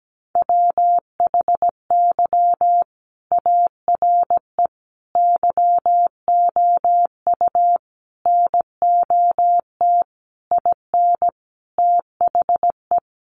Morse code, heard as a keyed tone, about 17 wpm, 700 Hz. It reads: WHY ARE YOU NOT IN THE
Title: Sentences from Top 1400 Words 17wpm